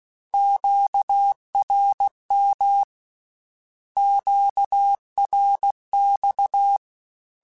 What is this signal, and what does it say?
Morse code, heard as a keyed tone, about 16 wpm, 800 Hz. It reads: QRM QRX
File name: Q Calls / Q Calls-mp3s / Q Calls Twice @15